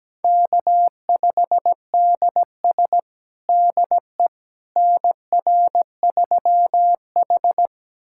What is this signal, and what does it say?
Morse code, heard as a keyed tone, about 17 wpm, 700 Hz. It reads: K5DS DE NR3H
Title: USA Callsign DE Callsign - 4 Characters - 17wpm